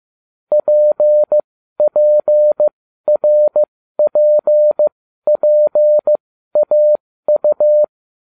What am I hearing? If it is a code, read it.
Morse code, heard as a keyed tone, about 15 wpm, spaced as for 12 wpm, 600 Hz. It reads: PPRPPAU